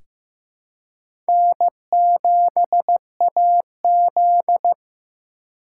Morse code, heard as a keyed tone, about 16 wpm, 700 Hz. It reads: N7AZ
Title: K1USN SST QSO Practice - 15wpm